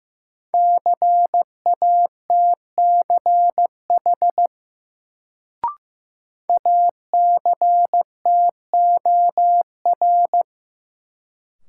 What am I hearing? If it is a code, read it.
Morse code, heard as a keyed tone, about 15 wpm, 700 Hz. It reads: CATCH ACTOR
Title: Sentences from Top 1500 Words - New Word Drill Alt - 15wpm